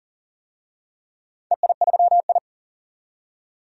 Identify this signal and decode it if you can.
Morse code, heard as a keyed tone, about 39 wpm, 700 Hz. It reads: EI3I
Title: DXCC-100 Call Signs - 4 Characters - 40wpm